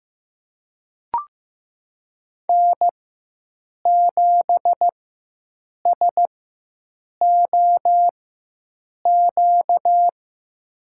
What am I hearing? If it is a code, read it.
Morse code, heard as a keyed tone, about 15 wpm, spaced as for 7 wpm, 700 Hz. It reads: N7SOQ
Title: USA Call Signs - 5 Characters - 4x Character Spacing - 15wpm